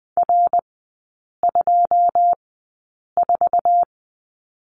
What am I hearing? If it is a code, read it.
Morse code, heard as a keyed tone, about 20 wpm, spaced as for 8 wpm, 700 Hz. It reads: R24
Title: 081 Lesson 14e - Number 2 - Three Characters 20wpm